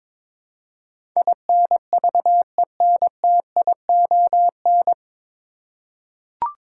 Morse code, heard as a keyed tone, about 23 wpm, 700 Hz. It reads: INVENTION
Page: Morse Code Ninja practice 22wpm